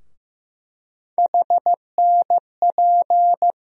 Morse code, heard as a keyed tone, about 15 wpm, 700 Hz. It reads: HNP
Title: USA Call Sign Postfixes - 3 Letters 15wpm